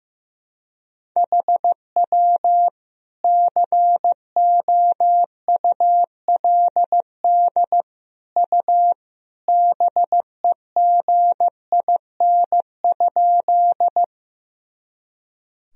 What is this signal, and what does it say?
Morse code, heard as a keyed tone, about 15 wpm, 700 Hz. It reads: HW COULD U BEGIN?
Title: Sentences from Top 300 Words and Abbreviations 15wpm